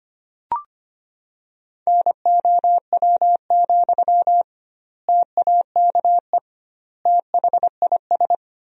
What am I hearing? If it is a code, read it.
Morse code, heard as a keyed tone, about 25 wpm, 700 Hz. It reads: NOW, TAKE THIS
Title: 2x Word-Spacing - Sentences from Top 400 Words - 25wpm